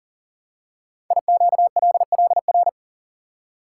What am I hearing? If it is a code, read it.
Morse code, heard as a keyed tone, about 38 wpm, 690 Hz. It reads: IQLLR
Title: Code Groups - 5 Letters - 40wpm